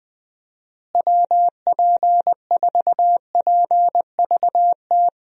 Morse code, heard as a keyed tone, about 20 wpm, 700 Hz. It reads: WP4PVT
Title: USA Call Signs - 6 Characters - Repeated 2x - 20wpm